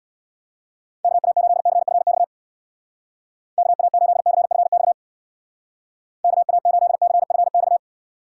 Morse code, heard as a keyed tone, about 63 wpm, 700 Hz. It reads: KA8CFX KA8CFX KA8CFX